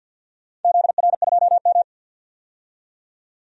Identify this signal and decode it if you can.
Morse code, heard as a keyed tone, about 46 wpm, 700 Hz. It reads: ZR2M
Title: DXCC-100 Call Signs - 4 Characters - 50wpm